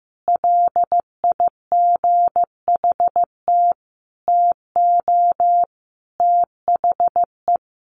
Morse code, heard as a keyed tone, about 15 wpm, 700 Hz. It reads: LIGHT TO THE
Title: Sentences from Top 1500 Words 15wpm